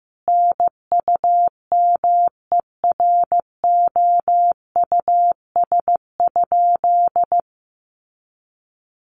Morse code, heard as a keyed tone, about 15 wpm, 700 Hz. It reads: NUMEROUS?
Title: Sentences from Top 1400 Words 15wpm